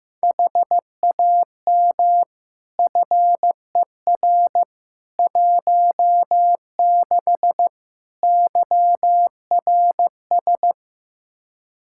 Morse code, heard as a keyed tone, about 15 wpm, 700 Hz. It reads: HAM FER 16 YRS
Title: K1USN SST QSO Practice 15wpm